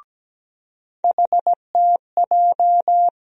Morse code, heard as a keyed tone, about 17 wpm, 700 Hz. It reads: HTJ
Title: USA Call Sign Postfixes - 3 Letters 17wpm